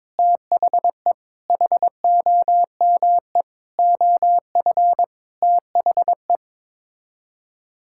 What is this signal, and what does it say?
Morse code, heard as a keyed tone, about 21 wpm, 700 Hz. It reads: THE HOME OF THE
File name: Sets of 9 Words - 22wpm